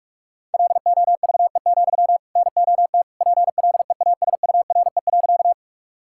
Morse code, heard as a keyed tone, about 46 wpm, 700 Hz. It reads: LOVE, NOT PLEASURE.
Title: Sentences from Top 1600 Words 45wpm